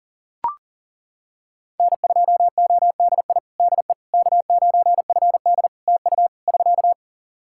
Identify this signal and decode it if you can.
Morse code, heard as a keyed tone, about 40 wpm, 700 Hz. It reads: N2ODI DE K9FD TU <SK>
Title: Practice QSO - Exchange 4 - Ending - 40wpm